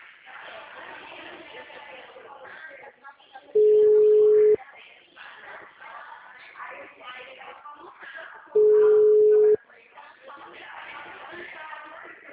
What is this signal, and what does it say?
Ringback tone (Europe, 425 Hz), with speech.
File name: out- -2045- - - .16223